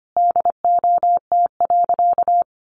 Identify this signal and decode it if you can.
Morse code, heard as a keyed tone, about 25 wpm, 700 Hz. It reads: DOT.
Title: Sentences from Top 1500 Words 25wpm